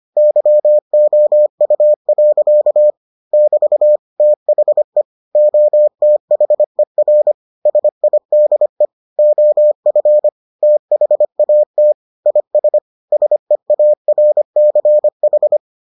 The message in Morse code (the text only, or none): YOU. <BT> THE OTHER SIDE OF THAT IS SEARCH